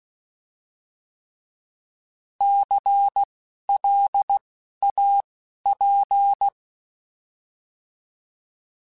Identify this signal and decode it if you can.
Morse code, heard as a keyed tone, about 16 wpm, spaced as for 11 wpm, 800 Hz. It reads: CLAP
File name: Alphabetical / Alphabetical mp3s / Cs @10